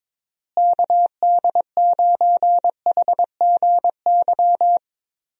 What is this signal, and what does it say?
Morse code, heard as a keyed tone, about 22 wpm, 700 Hz. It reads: KD9HGY
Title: USA Call Signs - 6 Characters - Repeated 2x - 22wpm